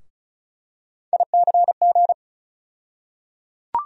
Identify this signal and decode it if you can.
Morse code, heard as a keyed tone, about 34 wpm, 700 Hz. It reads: ICG
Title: USA Call Sign Postfixes - 3 Letters 35wpm